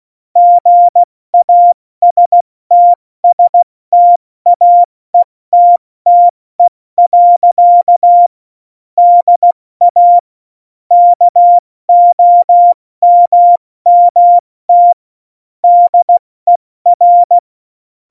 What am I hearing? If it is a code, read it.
Morse code, heard as a keyed tone, about 16 wpm, 700 Hz. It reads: GASTSTAETTE. DA KOMMT DER
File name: Morsekurs / Witze / Witz 01. Oktober 2019, 70 F80